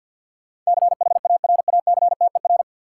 Morse code, heard as a keyed tone, about 53 wpm, 700 Hz. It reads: CHARACTER
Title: Sentences from Top 1200 Words - New Word Drill - 50wpm